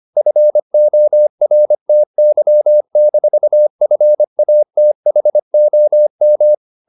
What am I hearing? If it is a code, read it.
Morse code, heard as a keyed tone, about 25 wpm, 600 Hz. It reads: FORTY-FATHOM
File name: the old man and the sea part 1 25wpm